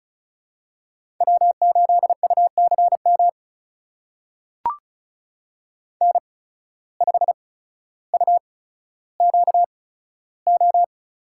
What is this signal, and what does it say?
Morse code, heard as a keyed tone, about 36 wpm, spaced as for 15 wpm, 700 Hz. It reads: W8UCM N5UQO
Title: USA Call Signs - 5 Characters - 8x Character Spacing - 35wpm